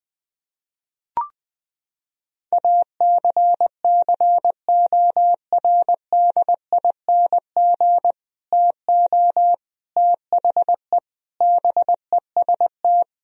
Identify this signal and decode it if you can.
Morse code, heard as a keyed tone, about 20 wpm, 700 Hz. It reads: ACCORDING TO THE BEST
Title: Sentences from Top 1000 Words - Review Part 4 - 20wpm